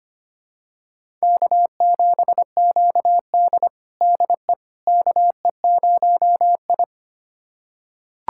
Morse code, heard as a keyed tone, about 25 wpm, 700 Hz. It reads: K7QD DE KE0I
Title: USA Callsign DE Callsign - 4 Characters - 25wpm